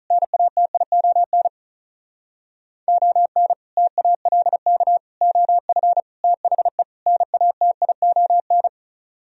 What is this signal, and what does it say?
Morse code, heard as a keyed tone, about 46 wpm, 700 Hz. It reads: NATION ON TALK OF THE NATION